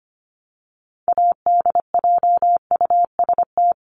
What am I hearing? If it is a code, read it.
Morse code, heard as a keyed tone, about 25 wpm, 700 Hz. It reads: ADJUST
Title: Top 1501-1750 Words 25wpm